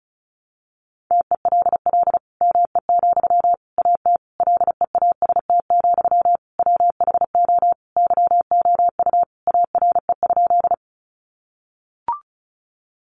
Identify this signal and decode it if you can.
Morse code, heard as a keyed tone, about 34 wpm, 700 Hz. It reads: TELL ME, AT LEAST, WHO YOU ARE?
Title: Sentences from Top 700 Words 35wpm